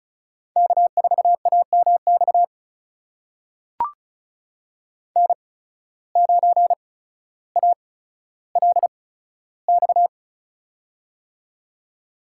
Morse code, heard as a keyed tone, about 36 wpm, spaced as for 15 wpm, 700 Hz. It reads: K4AMX N9ALX
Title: USA Call Signs - 5 Characters - 8x Character Spacing - 35wpm